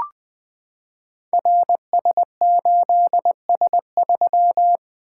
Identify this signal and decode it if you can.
Morse code, heard as a keyed tone, about 20 wpm, 700 Hz. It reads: RS8S3
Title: Code Groups - 5 Letter:Numbers - 20wpm